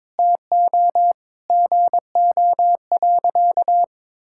Morse code, heard as a keyed tone, about 22 wpm, 700 Hz. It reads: TO GO.
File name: practice / 22wpm / Sentences with 4-Word Phrases - 22wpm